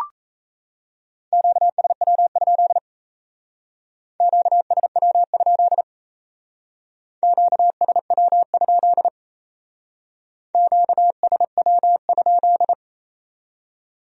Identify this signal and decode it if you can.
Morse code, heard as a keyed tone, about 34 wpm, 700 Hz. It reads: QSW? QSW? QSW? QSW?
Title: Speed-Racing Q-Codes - 28wpm